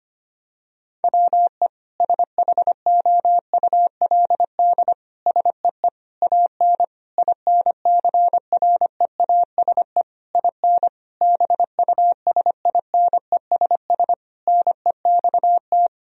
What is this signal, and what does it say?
Morse code, heard as a keyed tone, about 25 wpm, 700 Hz. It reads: WE SHOULD SEE AN INCREASE IN BUSINESS NEXT
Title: Sentences from Top 1000 Words - Review Part 4 - 25wpm